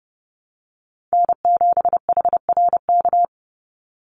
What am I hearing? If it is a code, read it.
Morse code, heard as a keyed tone, about 30 wpm, 700 Hz. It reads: N7HRK
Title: Call Signs Repeated 2x - 30wpm